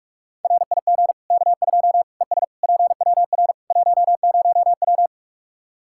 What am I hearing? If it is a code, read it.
Morse code, heard as a keyed tone, about 41 wpm, 700 Hz. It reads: RIG K2 ES PWR 10W
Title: Practice QSO - Exchange 2 - Info - 45wpm